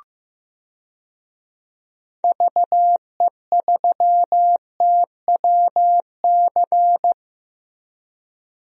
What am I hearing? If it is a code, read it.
Morse code, heard as a keyed tone, about 15 wpm, 700 Hz. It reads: VE3TWC
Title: DXCC-100 Call Signs - 6 Characters - Repeated 3x - 15wpm